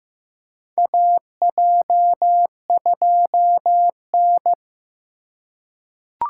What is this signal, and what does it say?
Morse code, heard as a keyed tone, about 15 wpm, 700 Hz. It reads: AJ2N
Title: USA Call Signs - 4 Characters - 2x Character Spacing - 15wpm